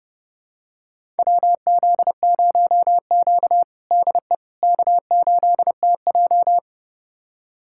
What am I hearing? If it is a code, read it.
Morse code, heard as a keyed tone, about 30 wpm, 700 Hz. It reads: WZ0Q DE K8TJ
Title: USA Callsign DE Callsign - 4 Characters - 30wpm